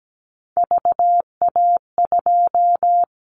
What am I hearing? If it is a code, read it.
Morse code, heard as a keyed tone, about 17 wpm, 700 Hz. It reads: VA2
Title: Callsign Prefixes for Top 100 DXCC - 17wpm